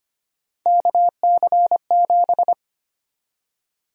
Morse code, heard as a keyed tone, about 25 wpm, 700 Hz. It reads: KC7
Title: USA Call Sign Prefixes - 25wpm